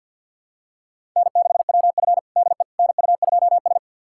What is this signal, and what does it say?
Morse code, heard as a keyed tone, about 51 wpm, 700 Hz. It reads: N6WF DE NU2S
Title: USA Callsign DE Callsign - 4 Characters - 50wpm